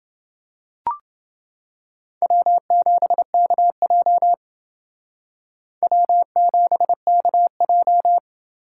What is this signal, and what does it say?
Morse code, heard as a keyed tone, about 28 wpm, 700 Hz. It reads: W7KJ W7KJ